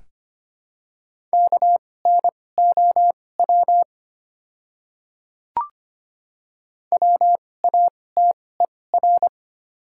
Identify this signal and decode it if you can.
Morse code, heard as a keyed tone, about 25 wpm, spaced as for 18 wpm, 700 Hz. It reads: KNOW WATER
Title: 2x Character Spacing - Top 100 Words - 25wpm